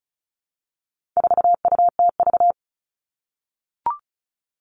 Morse code, heard as a keyed tone, about 34 wpm, 700 Hz. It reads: 4UTV
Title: All Cut Numbers - 4-Digit 35wpm